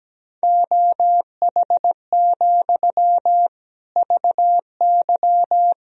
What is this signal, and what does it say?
Morse code, heard as a keyed tone, about 17 wpm, 700 Hz. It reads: OH, VY
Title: Sentences from Top 500 Words and Abbreviations 17wpm